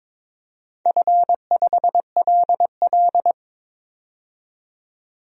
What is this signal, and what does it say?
Morse code, heard as a keyed tone, about 22 wpm, 700 Hz. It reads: F5LL
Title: DXCC-100 Call Signs - 4 Characters - 22wpm